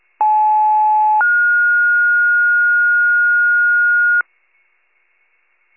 Two-tone page, heard as a one-second tone, then a three-second tone, about 860 then 1430 Hz, with noise. It is heard from 0.2 to 4.2 s.